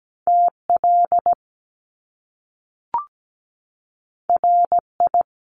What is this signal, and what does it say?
Morse code, heard as a keyed tone, about 17 wpm, 700 Hz. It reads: TL RI
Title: Top Two Letter Combinations in Words 17wpm